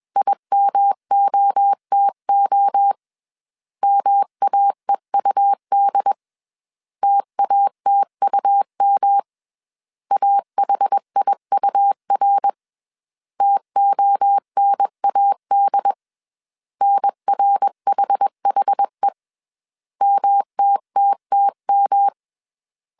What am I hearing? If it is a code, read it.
Morse code, heard as a keyed tone, about 21 wpm, 800 Hz. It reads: IMOTO MAEUD TATUM AHIUR TONAD NRHHE MTTTM